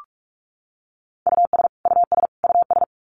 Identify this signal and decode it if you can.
Morse code, heard as a keyed tone, about 44 wpm, 700 Hz. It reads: US US US
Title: Top 300 Words Repeated 45wpm